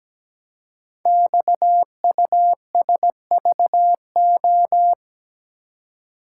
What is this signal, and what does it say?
Morse code, heard as a keyed tone, about 17 wpm, 700 Hz. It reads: XUSVO
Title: Code Groups - 5 Letter:Numbers - 17wpm